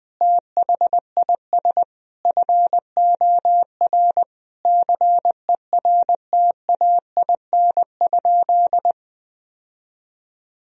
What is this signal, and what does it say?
Morse code, heard as a keyed tone, about 20 wpm, 700 Hz. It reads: THIS FOR CERTAIN?
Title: Sentences from Top 1000 Words - Review Part 2 - 20wpm